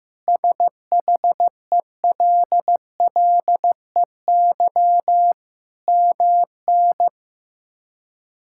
Morse code, heard as a keyed tone, about 15 wpm, 700 Hz. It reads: SHELLEY MN
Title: K1USN SST QSO Practice - 15wpm